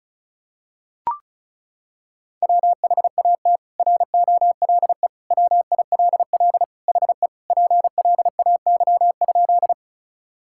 Morse code, heard as a keyed tone, about 34 wpm, 700 Hz. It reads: WHAT ROLE WILL HE PLAY?
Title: Sentences from Top 1000 Words - Review Part 6 - 35wpm